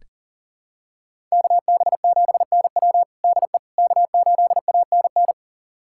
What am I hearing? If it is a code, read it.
Morse code, heard as a keyed tone, about 37 wpm, 700 Hz. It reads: KB7NW DE K8ANN
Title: USA Callsign DE Callsign - 5 Characters - 40wpm